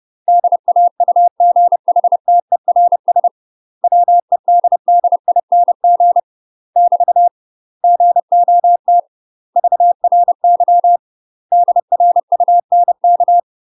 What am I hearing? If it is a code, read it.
Morse code, heard as a keyed tone, about 30 wpm, 700 Hz. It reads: DAUGHTERS WEDDING <BT> GOT VRY DRUNK
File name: UK MASTER QSOs TRAINING / 30WPM / UK MASTER QSOs 0040